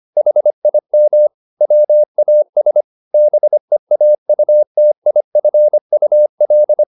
Morse code, heard as a keyed tone, about 25 wpm, 600 Hz. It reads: HIM WAS BEAUTIFUL